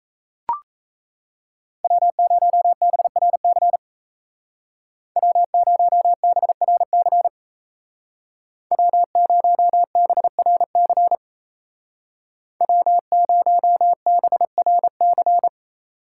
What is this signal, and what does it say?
Morse code, heard as a keyed tone, about 34 wpm, 700 Hz. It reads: W0BRC W0BRC W0BRC W0BRC
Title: Speed Racing - USA 1x3 Call Signs 28wpm